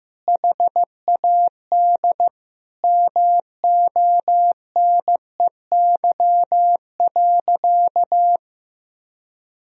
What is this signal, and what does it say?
Morse code, heard as a keyed tone, about 15 wpm, 700 Hz. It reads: HAD MONEY.
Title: Sentences from Top 500 Words and Abbreviations 15wpm